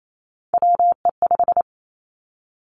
Morse code, heard as a keyed tone, about 28 wpm, 700 Hz. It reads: WE5
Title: Three Challenging Letter-Number Combinations 28wpm